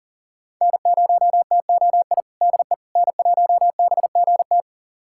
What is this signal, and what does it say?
Morse code, heard as a keyed tone, about 40 wpm, 700 Hz. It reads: N0TOI DE N1BGT
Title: USA Callsign DE Callsign - 5 Characters - 40wpm